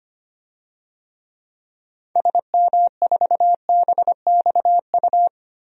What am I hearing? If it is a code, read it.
Morse code, heard as a keyed tone, about 25 wpm, 700 Hz. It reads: SM4BXU
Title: DXCC-100 Call Signs - 6 Characters - Repeated 3x - 25wpm